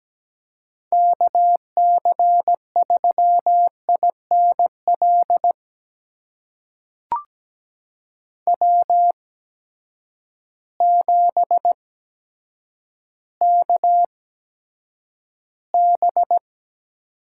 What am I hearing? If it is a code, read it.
Morse code, heard as a keyed tone, about 17 wpm, spaced as for 8 wpm, 700 Hz. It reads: KC3INL W7KB
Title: USA Call Signs - 8x Character Spacing - 17wpm